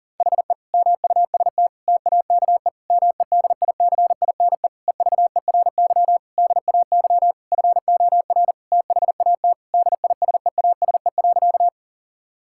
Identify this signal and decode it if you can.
Morse code, heard as a keyed tone, about 37 wpm, 700 Hz. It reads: HE MUST TAKE MEDICINE EVERY DAY FOR THAT DISEASE.
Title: Sentences from Top 1000 Words - Review Part 6 - 40wpm